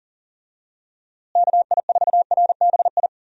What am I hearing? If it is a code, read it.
Morse code, heard as a keyed tone, about 40 wpm, 700 Hz. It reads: KI4RBI
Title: USA Call Signs - 6 Characters - Repeated 2x - 40wpm